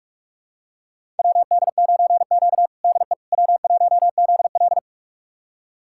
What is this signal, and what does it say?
Morse code, heard as a keyed tone, about 41 wpm, 700 Hz. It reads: WD9Q DE W1ZL
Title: USA Callsign DE Callsign - 4 Characters - 45wpm